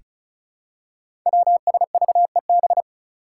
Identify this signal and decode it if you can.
Morse code, heard as a keyed tone, about 36 wpm, 700 Hz. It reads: WSVEB
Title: Five Challenging Letter-Number Combinations 35wpm